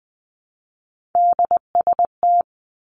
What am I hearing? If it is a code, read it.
Morse code, heard as a keyed tone, about 20 wpm, 700 Hz. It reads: DST
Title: Three Challenging Letter-Number Combinations 20wpm